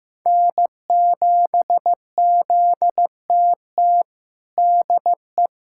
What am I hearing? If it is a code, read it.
Morse code, heard as a keyed tone, about 15 wpm, 700 Hz. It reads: N7ZTT DE
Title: USA Callsign DE Callsign - 5 Characters - 15wpm